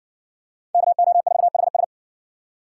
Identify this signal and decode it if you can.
Morse code, heard as a keyed tone, about 53 wpm, 700 Hz. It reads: KO4HS